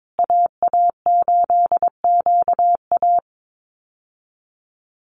Morse code, heard as a keyed tone, about 22 wpm, 700 Hz. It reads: AA8QA
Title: USA Call Signs Repeated 3x 22wpm